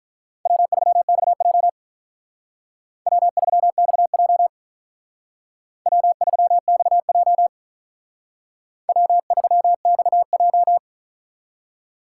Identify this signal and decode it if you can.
Morse code, heard as a keyed tone, about 45 wpm, 700 Hz. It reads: W3XJ W3XJ W3XJ W3XJ